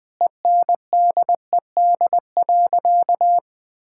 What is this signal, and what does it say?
Morse code, heard as a keyed tone, about 20 wpm, 700 Hz. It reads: ENDED.
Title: Sentences with 2-Word Phrases - Spoken Beginning - 20wpm